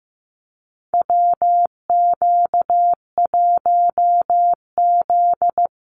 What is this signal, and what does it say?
Morse code, heard as a keyed tone, about 15 wpm, 700 Hz. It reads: WQ1Z
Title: USA Call Signs Repeated 2x 15wpm